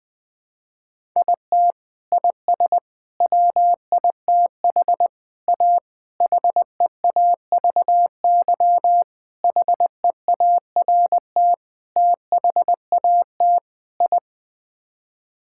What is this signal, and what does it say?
Morse code, heard as a keyed tone, about 21 wpm, 700 Hz. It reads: IT IS WITH A HEAVY HEART THAT I
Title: Sets of 8 Words - 20wpm